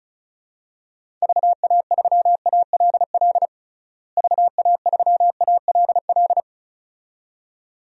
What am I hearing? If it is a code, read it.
Morse code, heard as a keyed tone, about 36 wpm, 700 Hz. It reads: VA3ALL VA3ALL
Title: DXCC-100 Call Signs - 6 Characters - Repeated 2x - 35wpm